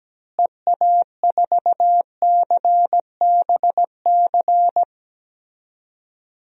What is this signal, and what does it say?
Morse code, heard as a keyed tone, about 17 wpm, 700 Hz. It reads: EA4CBC
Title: DXCC-100 Call Signs - 6 Characters - Repeated 2x - 17wpm